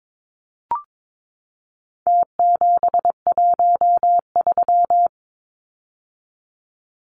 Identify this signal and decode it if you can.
Morse code, heard as a keyed tone, about 22 wpm, 700 Hz. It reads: T713
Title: Numbers - Contest Style - 4-Digit 22wpm